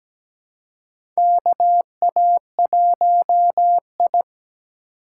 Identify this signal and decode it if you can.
Morse code, heard as a keyed tone, about 17 wpm, 700 Hz. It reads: KA1I
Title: USA Call Signs - 3x Character Spacing - 17wpm